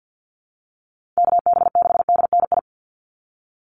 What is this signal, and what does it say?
Morse code, heard as a keyed tone, about 50 wpm, 700 Hz. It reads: KB6DNI
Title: USA Call Signs 50wpm